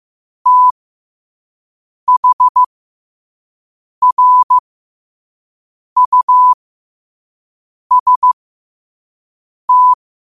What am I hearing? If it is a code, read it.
Morse code, heard as a keyed tone, about 15 wpm, spaced as for 5 wpm, 1000 Hz. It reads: THRUST
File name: qotdfiles / 5wpm